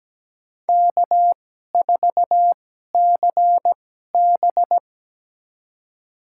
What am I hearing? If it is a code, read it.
Morse code, heard as a keyed tone, about 17 wpm, spaced as for 12 wpm, 700 Hz. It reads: K4CB
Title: USA Call Signs - 4 Characters - 2x Character Spacing - 17wpm